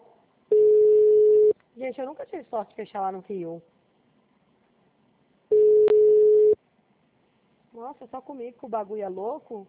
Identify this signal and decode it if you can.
Ringback tone (Europe, 425 Hz), with speech.
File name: out- -2013- - - .54888